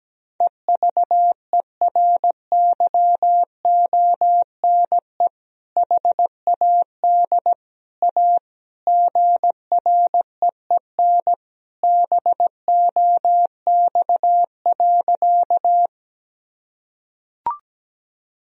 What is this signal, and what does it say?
Morse code, heard as a keyed tone, about 17 wpm, 700 Hz. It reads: EVERYONE HAD A GREEN BOX.
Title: Sentences from Top 1000 Words - Review Part 2 - 17wpm